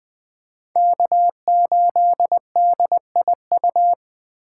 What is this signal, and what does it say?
Morse code, heard as a keyed tone, about 20 wpm, 700 Hz. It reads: K8DIU